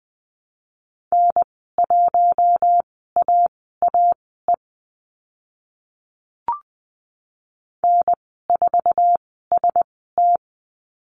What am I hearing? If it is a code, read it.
Morse code, heard as a keyed tone, about 20 wpm, spaced as for 14 wpm, 700 Hz. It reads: N1AAE N4ST
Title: 048 Lesson 9g - Number 4 - Call Signs 20wpm